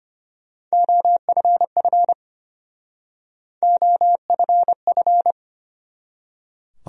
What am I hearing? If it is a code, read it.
Morse code, heard as a keyed tone, about 33 wpm, 700 Hz. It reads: OFF OFF